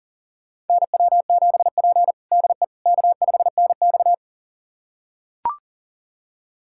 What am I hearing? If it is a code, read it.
Morse code, heard as a keyed tone, about 40 wpm, 700 Hz. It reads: NW7P DE K5NX
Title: USA Callsign DE Callsign - 4 Characters - 40wpm